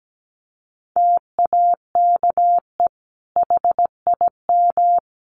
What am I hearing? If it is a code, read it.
Morse code, heard as a keyed tone, about 17 wpm, 700 Hz. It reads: TAKE HIM
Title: Sentences from Top 600 Words 17wpm